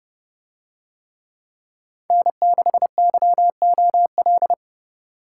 Morse code, heard as a keyed tone, about 29 wpm, 700 Hz. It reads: N6YOL